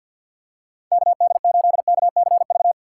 Morse code, heard as a keyed tone, about 48 wpm, 700 Hz. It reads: KD8KCV
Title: USA Call Signs - 6 Characters - 6x Character Spacing - 50wpm